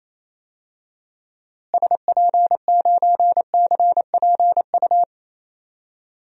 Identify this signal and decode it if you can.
Morse code, heard as a keyed tone, about 28 wpm, 700 Hz. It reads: SP9CPU
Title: DXCC-100 Call Signs - 6 Characters - Repeated 3x - 28wpm